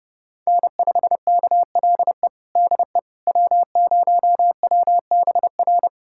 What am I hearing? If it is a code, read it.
Morse code, heard as a keyed tone, about 30 wpm, 700 Hz. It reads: N5KLE DE W0WBR
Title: USA Callsign DE Callsign - 5 Characters - 30wpm